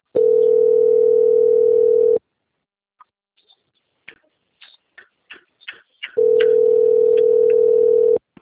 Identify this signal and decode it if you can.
Ringback tone (North America), 440 + 480 Hz.